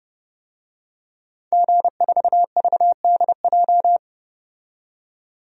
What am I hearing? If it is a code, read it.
Morse code, heard as a keyed tone, about 30 wpm, 700 Hz. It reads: G4VDJ
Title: DXCC-100 Call Signs - 5 Characters - 30wpm